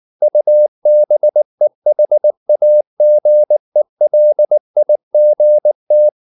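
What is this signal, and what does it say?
Morse code, heard as a keyed tone, about 19 wpm, 600 Hz. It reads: UBEHAGELIGT
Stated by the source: Slow (QRS) Morse practice at 18 wpm, CW med Gnister.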